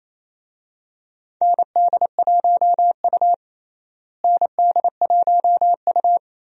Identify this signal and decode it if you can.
Morse code, heard as a keyed tone, about 28 wpm, 700 Hz. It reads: ND1U ND1U
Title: USA Call Signs - 4 Characters - Repeated 2x - 28wpm